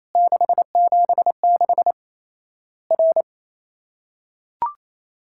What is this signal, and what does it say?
Morse code, heard as a keyed tone, about 28 wpm, 700 Hz. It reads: 676 R
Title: Worked All Europe DX Contest - QTC Exchange - 28wpm